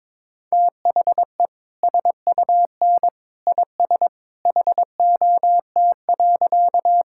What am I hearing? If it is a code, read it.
Morse code, heard as a keyed tone, about 22 wpm, 700 Hz. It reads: THE SUN IS HOT.
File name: practice / 22wpm / Easy Sentences with Hints - 22wpm